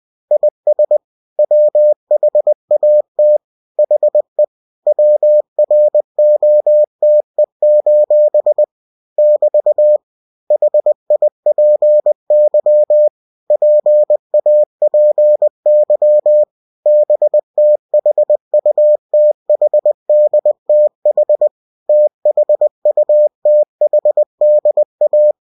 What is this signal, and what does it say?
Morse code, heard as a keyed tone, about 20 wpm, 600 Hz. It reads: IS WHAT HE WROTE: <BT> HIPY PAPY BTHUTHDTH THUTHDA
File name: cw-books / winnie-the-pooh / winnie-20wpm-page-0052